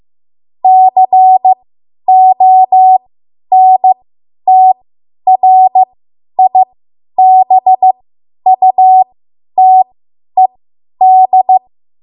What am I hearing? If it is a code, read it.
Morse code, heard as a keyed tone, about 15 wpm, spaced as for 10 wpm, 750 Hz. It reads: CONTRIBUTED
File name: Morse / 10WPM